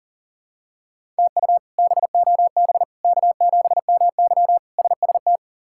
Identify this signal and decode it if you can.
Morse code, heard as a keyed tone, about 40 wpm, 700 Hz. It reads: TU BOB K7MY SST
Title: K1USN SST QSO Practice - 40wpm